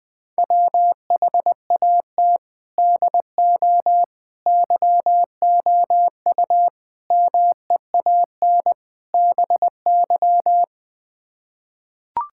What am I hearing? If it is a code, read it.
Morse code, heard as a keyed tone, about 20 wpm, 700 Hz. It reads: WHAT DO YOU MEAN BY
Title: Build A Phrase - 5-Word Phrases - 20wpm